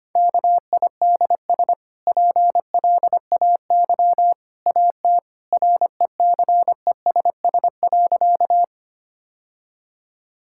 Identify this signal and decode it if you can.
Morse code, heard as a keyed tone, about 25 wpm, 690 Hz. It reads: KIDS PLAY AT RECESS.
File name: Easy Sentences with Hints - 25wpm